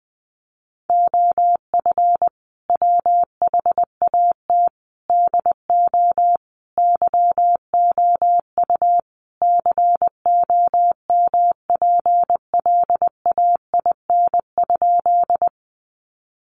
Morse code, heard as a keyed tone, about 19 wpm, 700 Hz. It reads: OF WHAT DO YOU COMPLAIN?
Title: Sentences from Top 1300 Words 20wpm